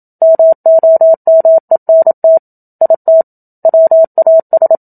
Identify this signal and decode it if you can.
Morse code, heard as a keyed tone, about 27 wpm, 650 Hz. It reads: MOMENT IT WAS
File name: Book Two / 01-Chapter 1 - 27 WPM